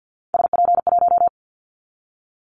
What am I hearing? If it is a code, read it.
Morse code, heard as a keyed tone, about 50 wpm, 700 Hz. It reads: SP1
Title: Callsign Prefixes for Top 25 DXCC - 50wpm